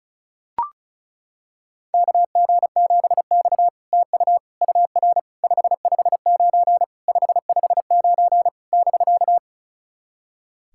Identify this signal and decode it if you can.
Morse code, heard as a keyed tone, about 35 wpm, 700 Hz. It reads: KG7X TU UR 559 559 <BK>
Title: SOTA QSO Practice 35wpm